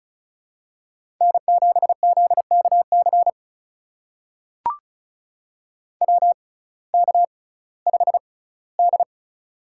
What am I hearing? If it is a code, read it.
Morse code, heard as a keyed tone, about 35 wpm, spaced as for 19 wpm, 700 Hz. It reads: N7ZKC WK5D
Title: USA Call Signs - 6x Character Spacing - 35wpm